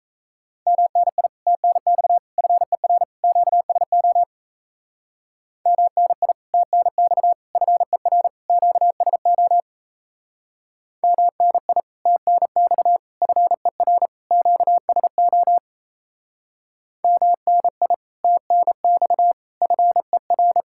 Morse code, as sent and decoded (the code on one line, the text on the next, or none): -- -. .. / - -. -..- / ..-. .-. / --.- ... --- / -- -. .. / - -. -..- / ..-. .-. / --.- ... --- / -- -. .. / - -. -..- / ..-. .-. / --.- ... --- / -- -. .. / - -. -..- / ..-. .-.
MNI TNX FER QSO MNI TNX FER QSO MNI TNX FER QSO MNI TNX FER